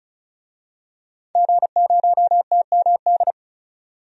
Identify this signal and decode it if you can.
Morse code, heard as a keyed tone, about 36 wpm, 700 Hz. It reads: G0TMD
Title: DXCC-100 Call Signs - 5 Characters - 35wpm